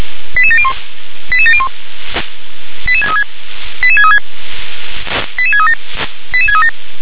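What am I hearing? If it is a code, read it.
Five-tone selective call (ZVEI), heard as quick runs of five tones, tones of about 70 ms. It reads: 88701 88701 88636 88636 88637 88637
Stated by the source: received on SDR, ZVEI five-tone selective call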